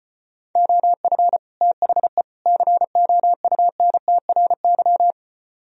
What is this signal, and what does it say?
Morse code, heard as a keyed tone, about 33 wpm, 700 Hz. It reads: OF THE COUNTRY